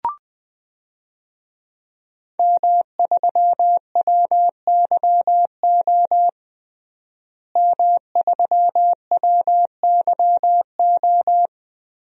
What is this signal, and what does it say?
Morse code, heard as a keyed tone, about 20 wpm, 700 Hz. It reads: M3WYO M3WYO
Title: DXCC-100 Call Signs - 5 Characters - Repeated 3x - 20wpm